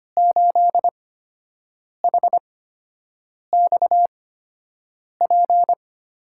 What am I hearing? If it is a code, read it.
Morse code, heard as a keyed tone, about 25 wpm, spaced as for 7 wpm, 700 Hz. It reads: 8HXP
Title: USA Call Signs - 6 Characters - 8x Character Spacing - 25wpm